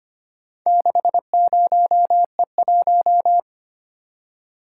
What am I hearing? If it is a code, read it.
Morse code, heard as a keyed tone, about 25 wpm, 700 Hz. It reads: 60E1
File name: practice / 25wpm / Four Letter-Number Combinations - 25wpm